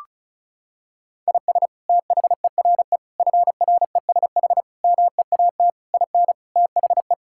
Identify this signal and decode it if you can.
Morse code, heard as a keyed tone, about 36 wpm, 700 Hz. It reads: IS THERE FRESH MEAT IN THE
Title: Sentences from Top 1600 Words 35wpm